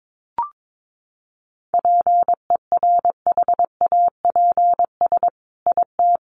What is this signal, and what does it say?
Morse code, heard as a keyed tone, about 22 wpm, 700 Hz. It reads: PERHAPS IT
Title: Sentences from Top 1500 Words 22wpm